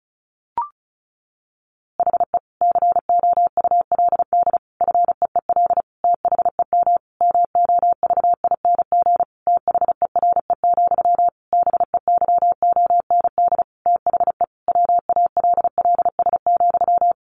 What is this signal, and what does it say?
Morse code, heard as a keyed tone, about 34 wpm, 700 Hz. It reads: HE COULD FEEL THEM MOVING THERE, BEYOND THE WALLS,
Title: A Sound Of Thunder 35wpm